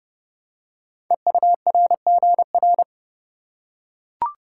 Morse code, heard as a keyed tone, about 30 wpm, 700 Hz. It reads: EURGR
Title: Code Groups - 5 Letters - 30wpm